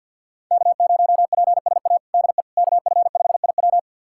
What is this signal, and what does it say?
Morse code, heard as a keyed tone, about 51 wpm, 700 Hz. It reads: K0PSA DE KF5IW
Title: USA Callsign DE Callsign - 5 Characters - 50wpm